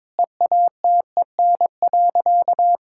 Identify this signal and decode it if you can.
Morse code, heard as a keyed tone, about 22 wpm, 700 Hz. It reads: EATEN.
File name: practice / 22wpm / Sentences with 2-Word Phrases - 22wpm